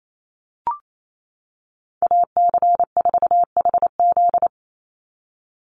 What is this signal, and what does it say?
Morse code, heard as a keyed tone, about 28 wpm, 700 Hz. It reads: AC4HZ
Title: USA Call Signs 28wpm